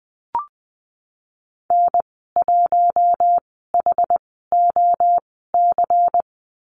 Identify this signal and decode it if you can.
Morse code, heard as a keyed tone, about 20 wpm, spaced as for 14 wpm, 700 Hz. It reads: N1HOC
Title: 128 Lesson 21g - Number 6 - Call Signs 20wpm